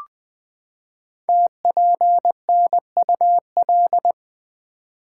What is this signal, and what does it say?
Morse code, heard as a keyed tone, about 20 wpm, 700 Hz. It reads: TPNUL
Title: Code Groups - 5 Letters - 20wpm